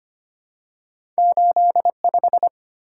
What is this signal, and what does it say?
Morse code, heard as a keyed tone, about 25 wpm, 700 Hz. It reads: 8 5
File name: Two Letter-Number Combinations - 25wpm